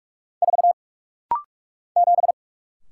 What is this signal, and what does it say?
Morse code, heard as a keyed tone, about 44 wpm, 700 Hz. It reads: 4 7
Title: Single Letter-Number Alt - Rapid-Fire 45wpm